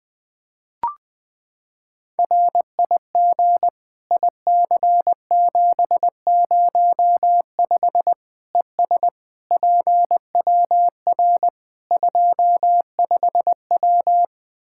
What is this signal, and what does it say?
Morse code, heard as a keyed tone, about 20 wpm, 700 Hz. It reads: RIG IC705 ES PWR 25W
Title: Practice QSO - Exchange 2 - Info - 20wpm